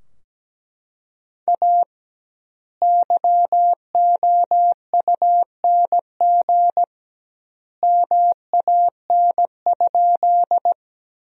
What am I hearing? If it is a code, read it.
Morse code, heard as a keyed tone, about 17 wpm, 700 Hz. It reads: A YOUNG MAN?
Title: 2x Word-Spacing - Sentences from Top 400 Words - 17wpm